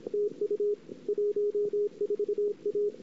Morse code, heard as a keyed tone, about 31 wpm, 410 Hz. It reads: TU 14 A
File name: CW / 2008-ss-cw-k5tr-op-n5rz -21:00